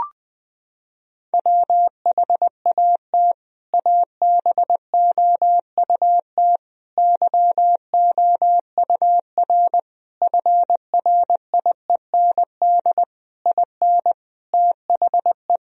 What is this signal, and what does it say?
Morse code, heard as a keyed tone, about 20 wpm, 700 Hz. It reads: WHAT ABOUT YOUR FRIEND IN THE
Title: Sentences from Top 1000 Words - Review Part 1 - 20wpm